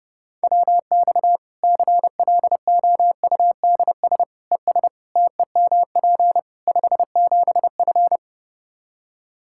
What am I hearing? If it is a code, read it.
Morse code, heard as a keyed tone, about 30 wpm, 700 Hz. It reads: WX CLOUDS ES TEMP 57F
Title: Practice QSO - Exchange 2 - Info - 30wpm